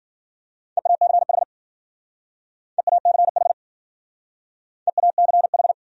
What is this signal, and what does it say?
Morse code, heard as a keyed tone, about 50 wpm, 700 Hz. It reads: EACH EACH EACH